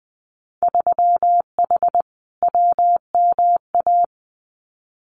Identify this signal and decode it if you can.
Morse code, heard as a keyed tone, about 20 wpm, 700 Hz. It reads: 3H WMA
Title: Winter Field Day Exchanges Encore 20wpm